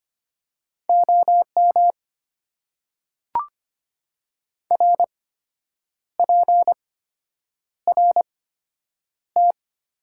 Morse code, heard as a keyed tone, about 25 wpm, spaced as for 8 wpm, 700 Hz. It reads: OM RPRT
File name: Top 90 QSO Elements - 8x Character Spacing - 25wpm